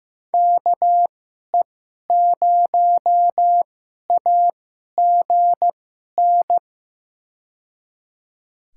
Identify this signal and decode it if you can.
Morse code, heard as a keyed tone, about 15 wpm, spaced as for 11 wpm, 700 Hz. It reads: KE0AGN
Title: USA Call Signs - 2x Character Spacing - 15wpm